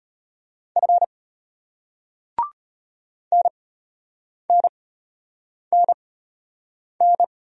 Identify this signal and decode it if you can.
Morse code, heard as a keyed tone, about 34 wpm, spaced as for 8 wpm, 700 Hz. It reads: F NNNN